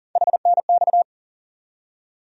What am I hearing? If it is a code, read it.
Morse code, heard as a keyed tone, about 40 wpm, 700 Hz. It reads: HNX
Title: USA Call Sign Postfixes - 3 Letters 40wpm